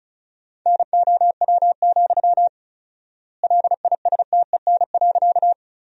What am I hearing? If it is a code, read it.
Morse code, heard as a keyed tone, about 34 wpm, 700 Hz. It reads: NOW, LISTEN.
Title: 4x Word-Spacing - Sentences from Top 400 Words - 35wpm